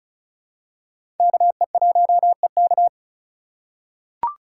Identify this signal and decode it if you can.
Morse code, heard as a keyed tone, about 36 wpm, 700 Hz. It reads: KE1EK
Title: USA Call Signs - 5 Characters - 4x Character Spacing - 35wpm